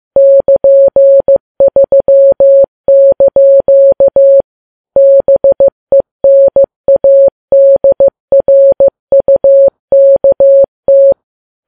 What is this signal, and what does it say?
Morse code, heard as a keyed tone, about 15 wpm, 550 Hz. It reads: <KN>3) BENADRUKT
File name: Files Web tls / Morse on9fv / morse15kps